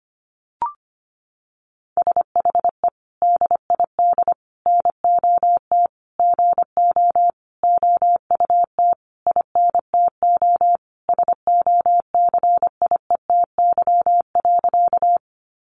Morse code, heard as a keyed tone, about 25 wpm, 700 Hz. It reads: SHE DID NOT GO OUT INTO SOCIETY.
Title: Sentences from Top 700 Words 25wpm